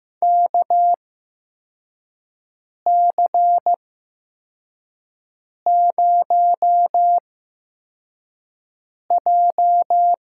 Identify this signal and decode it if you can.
Morse code, heard as a keyed tone, about 15 wpm, spaced as for 4 wpm, 700 Hz. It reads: KC0J
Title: USA Call Signs - 6 Characters - 8x Character Spacing - 15wpm